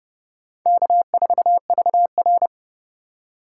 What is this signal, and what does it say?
Morse code, heard as a keyed tone, about 29 wpm, 700 Hz. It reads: K4VR